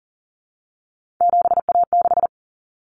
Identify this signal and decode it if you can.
Morse code, heard as a keyed tone, about 40 wpm, 700 Hz. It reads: 7A6
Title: Common Cut Numbers - 3-Digit 40wpm